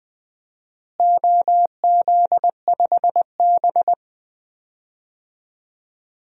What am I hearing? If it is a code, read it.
Morse code, heard as a keyed tone, about 20 wpm, 700 Hz. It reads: OZ5B
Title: DXCC-100 Call Signs - 4 Characters - Repeated 3x - 20wpm